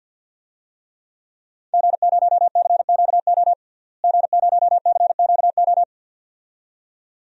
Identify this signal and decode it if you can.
Morse code, heard as a keyed tone, about 51 wpm, 700 Hz. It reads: G0CQO G0CQO
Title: DXCC-100 Call Signs - 5 Characters - Repeated 2x - 50wpm